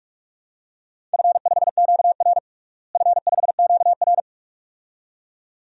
Morse code, heard as a keyed tone, about 41 wpm, 700 Hz. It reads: F5QR F5QR
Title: DXCC-100 Call Signs - 4 Characters - Repeated 2x - 45wpm